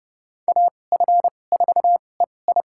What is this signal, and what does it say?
Morse code, heard as a keyed tone, about 30 wpm, spaced as for 22 wpm, 700 Hz. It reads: AF4EI
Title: USA Call Signs - 5 Characters - 2x Character Spacing - 30wpm